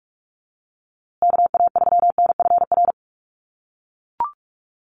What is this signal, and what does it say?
Morse code, heard as a keyed tone, about 44 wpm, 700 Hz. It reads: KA3NFR
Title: USA Call Signs Repeated 3x 45wpm